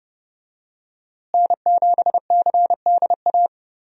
Morse code, heard as a keyed tone, about 31 wpm, 700 Hz. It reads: N7CDA
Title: USA Call Signs - 5 Characters - Repeated 2x - 30wpm